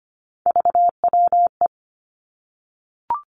Morse code, heard as a keyed tone, about 25 wpm, 700 Hz. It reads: VWE
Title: Three Challenging Letter-Number Combinations 25wpm